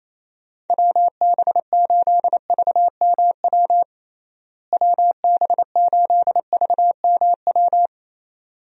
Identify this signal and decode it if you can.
Morse code, heard as a keyed tone, about 28 wpm, 700 Hz. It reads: WB8VMW WB8VMW
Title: USA Call Signs - 6 Characters - Repeated 2x - 28wpm